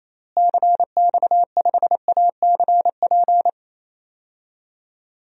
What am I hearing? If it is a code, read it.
Morse code, heard as a keyed tone, about 28 wpm, 700 Hz. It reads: CX5ACP
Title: DXCC-100 Call Signs - 6 Characters - Repeated 2x - 28wpm